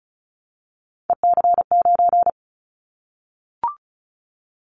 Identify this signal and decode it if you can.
Morse code, heard as a keyed tone, about 36 wpm, 700 Hz. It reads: EC9
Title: Callsign Prefixes for Top 100 DXCC - 35wpm